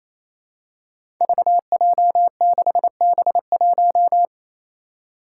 Morse code, heard as a keyed tone, about 27 wpm, 700 Hz. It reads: VJ6B1
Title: Five Challenging Letter-Number Combinations 28wpm